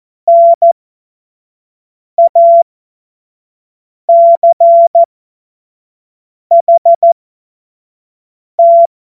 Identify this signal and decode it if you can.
Morse code, heard as a keyed tone, about 14 wpm, spaced as for 5 wpm, 680 Hz. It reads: NACHT
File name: Morsekurs / Witze / Witz 30. November 2021, 25 F70